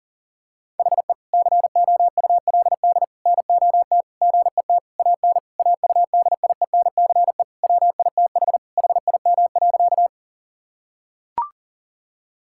Morse code, heard as a keyed tone, about 40 wpm, 700 Hz. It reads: HE COULD NOT GET AN AUDIENCE WITH HIM.
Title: Sentences from Top 1000 Words - Review Part 5 - 40wpm